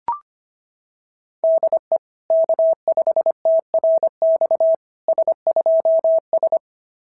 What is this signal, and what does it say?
Morse code, heard as a keyed tone, about 25 wpm, 660 Hz. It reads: DE K5TRX S2S
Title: SOTA QSO Practice 25wpm